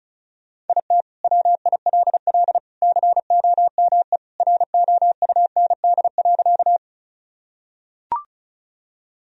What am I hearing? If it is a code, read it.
Morse code, heard as a keyed tone, about 36 wpm, 700 Hz. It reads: IT WILL COME ROUND.
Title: Sentences from Top 1000 Words - Review Part 1 - 35wpm